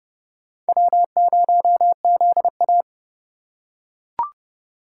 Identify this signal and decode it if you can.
Morse code, heard as a keyed tone, about 29 wpm, 700 Hz. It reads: W0ZA